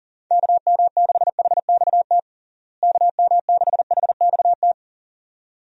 Morse code, heard as a keyed tone, about 37 wpm, 700 Hz. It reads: KM6HXT KM6HXT
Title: USA Call Signs - 6 Characters - Repeated 2x - 40wpm